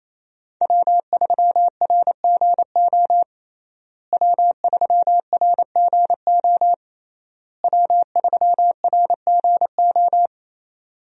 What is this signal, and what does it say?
Morse code, heard as a keyed tone, about 28 wpm, 700 Hz. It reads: W3RGO W3RGO W3RGO
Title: USA Call Signs - 5 Characters - Repeated 3x - 28wpm